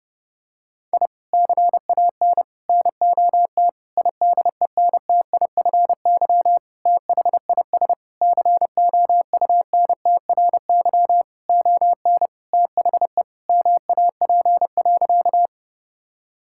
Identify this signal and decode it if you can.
Morse code, heard as a keyed tone, about 30 wpm, 700 Hz. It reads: I CAN NOT IDENTIFY THIS COUNTRY ON THE MAP.
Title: Sentences from Top 1000 Words - Review Part 6 - 30wpm